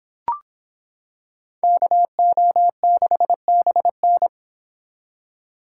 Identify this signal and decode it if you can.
Morse code, heard as a keyed tone, about 26 wpm, 700 Hz. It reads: KO6BN